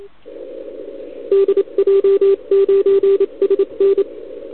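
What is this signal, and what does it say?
Morse code, heard as a keyed tone, about 28 wpm, 390 Hz. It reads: DJ9SN
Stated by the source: CW from the OKOM DX contest on 80 m, received with a TS480.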